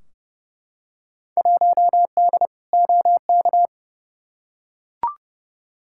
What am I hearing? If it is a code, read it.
Morse code, heard as a keyed tone, about 30 wpm, 700 Hz. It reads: 1D OK
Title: Field Day Exchanges - 30wpm